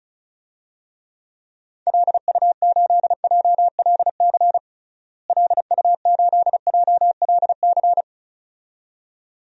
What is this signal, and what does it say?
Morse code, heard as a keyed tone, about 34 wpm, 700 Hz. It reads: LU8JLC LU8JLC
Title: DXCC-100 Call Signs - 6 Characters - Repeated 2x - 35wpm